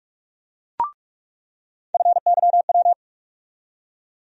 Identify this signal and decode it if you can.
Morse code, heard as a keyed tone, about 47 wpm, 700 Hz. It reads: FYW
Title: USA Call Sign Postfixes - 3 Letters 45wpm